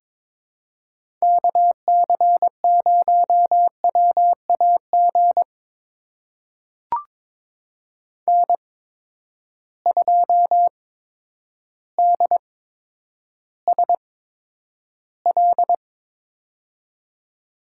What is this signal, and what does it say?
Morse code, heard as a keyed tone, about 22 wpm, spaced as for 10 wpm, 700 Hz. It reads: KC0WAG N2DSL
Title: USA Call Signs - 8x Character Spacing - 22wpm